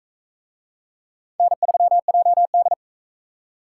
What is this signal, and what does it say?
Morse code, heard as a keyed tone, about 40 wpm, 700 Hz. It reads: N3JD